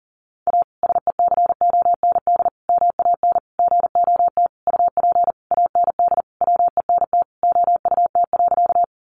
Morse code, heard as a keyed tone, about 40 wpm, 700 Hz. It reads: A SECOND MAN GOT UP AND WENT OUT.
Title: Sentences from Top 300 Words 40wpm